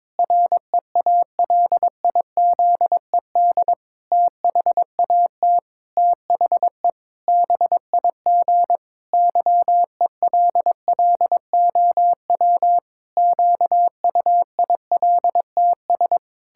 Morse code, heard as a keyed tone, about 22 wpm, 700 Hz. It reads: REALIZED THAT THE BIG YELLOW QUILTS
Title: Pangram Challenge v2 - 22wpm